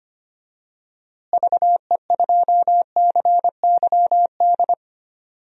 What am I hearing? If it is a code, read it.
Morse code, heard as a keyed tone, about 25 wpm, 700 Hz. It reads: VE2CYD
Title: DXCC-100 Call Signs - 6 Characters - Repeated 2x - 25wpm